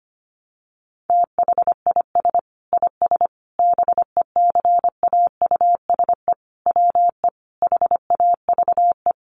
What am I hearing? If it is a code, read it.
Morse code, heard as a keyed tone, about 25 wpm, 700 Hz. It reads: THIS IS BECAUSE WE HAVE
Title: Sentences from Top 1200 Words 25wpm